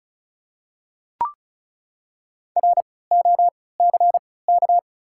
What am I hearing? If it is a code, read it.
Morse code, heard as a keyed tone, about 36 wpm, spaced as for 20 wpm, 690 Hz. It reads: ROCK
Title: 3x Character Spacing - Top 500 Words - 35wpm